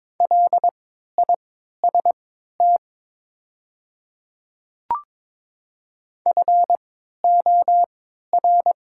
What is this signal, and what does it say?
Morse code, heard as a keyed tone, about 22 wpm, spaced as for 12 wpm, 700 Hz. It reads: LIST FOR
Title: 3x Character Spacing - Top 500 Words - 22wpm